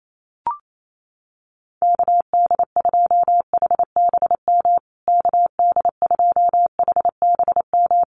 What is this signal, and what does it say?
Morse code, heard as a keyed tone, about 28 wpm, 700 Hz. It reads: KD2HBM KD2HBM
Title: USA Call Signs Repeated 3x 28wpm